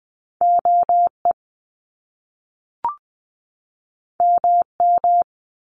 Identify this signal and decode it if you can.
Morse code, heard as a keyed tone, about 20 wpm, 700 Hz. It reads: OE MM
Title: Top Two Letter Combinations in Words 20wpm